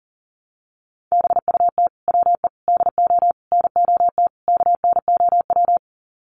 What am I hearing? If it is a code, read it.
Morse code, heard as a keyed tone, about 40 wpm, 700 Hz. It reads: BUT WE DO NOT KNOW
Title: Sets of 5 Words Encore2 40wpm